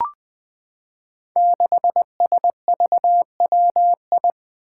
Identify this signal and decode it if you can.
Morse code, heard as a keyed tone, about 21 wpm, 700 Hz. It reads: 6SVWI
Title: Five Challenging Letter-Number Combinations 20wpm